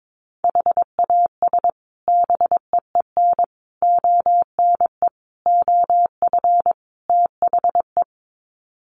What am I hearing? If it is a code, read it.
Morse code, heard as a keyed tone, about 22 wpm, 700 Hz. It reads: HAS BEEN ONE OF THE
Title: Sets of 5 Words Encore 22wpm